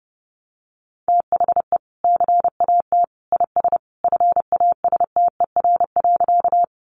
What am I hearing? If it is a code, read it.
Morse code, heard as a keyed tone, about 30 wpm, 700 Hz. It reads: THE CAT IS FASTER.
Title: Additional Easy Phrases 30wpm